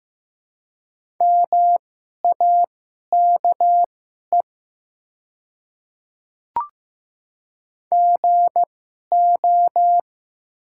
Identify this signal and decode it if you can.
Morse code, heard as a keyed tone, about 15 wpm, spaced as for 11 wpm, 700 Hz. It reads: MAKE GO
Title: 2x Character Spacing - Top 300 Words - 15wpm